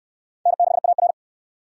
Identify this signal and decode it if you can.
Morse code, heard as a keyed tone, about 62 wpm, 700 Hz. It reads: N6AR